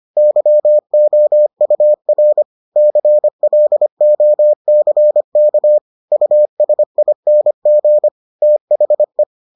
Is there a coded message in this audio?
Yes: Morse code from 0.2 to 9.2 s, heard as a keyed tone, about 25 wpm, 600 Hz. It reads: YOUR CLOCK USING THE